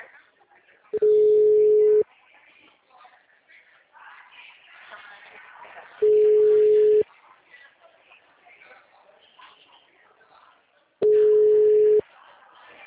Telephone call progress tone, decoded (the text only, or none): ringback tone